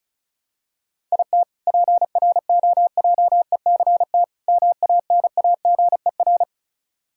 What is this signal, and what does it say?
Morse code, heard as a keyed tone, about 36 wpm, 700 Hz. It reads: IT PROJECT MANAGER
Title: K1USN SST QSO Practice 35wpm